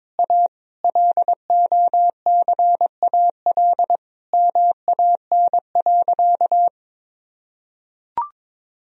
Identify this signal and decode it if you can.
Morse code, heard as a keyed tone, about 22 wpm, 700 Hz. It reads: A LOCAL MAN.
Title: Sentences from Top 1000 Words - Review Part 3 - 22wpm